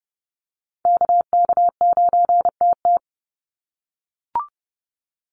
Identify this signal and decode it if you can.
Morse code, heard as a keyed tone, about 30 wpm, 700 Hz. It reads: KK9TT
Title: USA Call Signs 30wpm